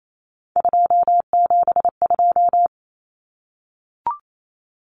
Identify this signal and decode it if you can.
Morse code, heard as a keyed tone, about 28 wpm, 700 Hz. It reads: 272
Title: Numbers - 3-Digit 28wpm